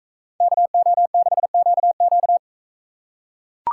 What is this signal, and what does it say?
Morse code, heard as a keyed tone, about 43 wpm, 700 Hz. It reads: KO6QQ